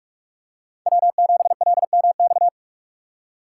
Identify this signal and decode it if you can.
Morse code, heard as a keyed tone, about 44 wpm, 700 Hz. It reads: W7LMX